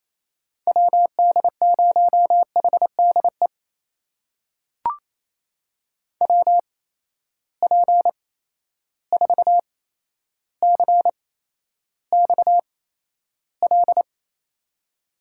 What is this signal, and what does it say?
Morse code, heard as a keyed tone, about 27 wpm, spaced as for 12 wpm, 700 Hz. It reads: WD0HDE WP4CXL
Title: USA Call Signs - 6 Characters - 8x Character Spacing - 28wpm